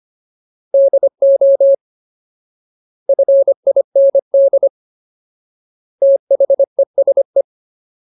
Morse code, heard as a keyed tone, about 25 wpm, 550 Hz. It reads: DO FIND THESE
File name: Ultimate Word List / 25WPM Ultimate